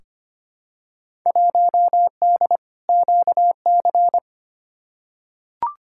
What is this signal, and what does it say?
Morse code, heard as a keyed tone, about 25 wpm, 700 Hz. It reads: 1D QC
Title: Field Day Exchanges Encore - 25wpm